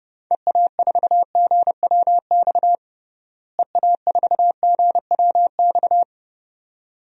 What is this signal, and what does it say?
Morse code, heard as a keyed tone, about 30 wpm, 700 Hz. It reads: EA4GWX EA4GWX
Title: DXCC-100 Call Signs - 6 Characters - Repeated 2x - 30wpm